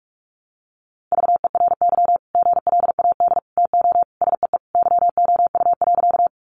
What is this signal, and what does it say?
Morse code, heard as a keyed tone, about 44 wpm, 700 Hz. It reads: VERY GLAD TO SEE YOU.
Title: Sentences from Top 800 Words 45wpm